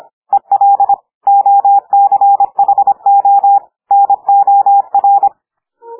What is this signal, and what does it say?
Morse code, heard as a keyed tone, about 26 wpm, 800 Hz. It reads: EL OCHO NOR
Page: Fast hand speed - over 25 wpm